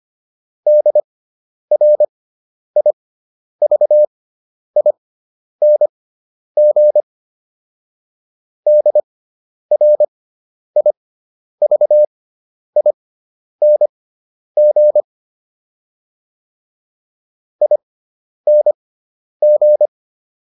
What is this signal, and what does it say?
Morse code, heard as a keyed tone, about 25 wpm, spaced as for 10 wpm, 600 Hz. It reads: DRIVING DRIVING ING